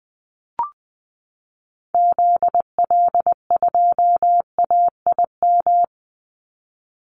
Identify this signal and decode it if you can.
Morse code, heard as a keyed tone, about 20 wpm, 700 Hz. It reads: ZL2AIM
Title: Call Signs Repeated 3x - 20wpm